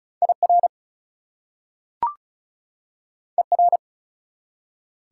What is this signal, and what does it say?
Morse code, heard as a keyed tone, about 36 wpm, 700 Hz. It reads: IR ER
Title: Prefixes and Suffixes 35wpm